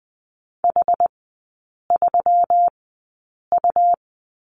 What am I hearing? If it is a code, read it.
Morse code, heard as a keyed tone, about 20 wpm, spaced as for 8 wpm, 700 Hz. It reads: H3U
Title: 121 Lesson 20e - Number 3 - Three Characters 20wpm